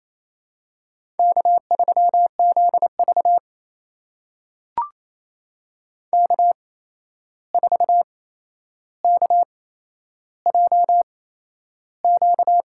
Morse code, heard as a keyed tone, about 28 wpm, spaced as for 11 wpm, 700 Hz. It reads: K3ZV K4KJQ
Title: USA Call Signs - 8x Character Spacing - 28wpm